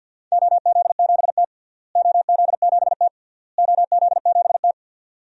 Morse code, heard as a keyed tone, about 53 wpm, 700 Hz. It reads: OZ7T OZ7T OZ7T
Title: DXCC-100 Call Signs - 4 Characters - Repeated 3x - 50wpm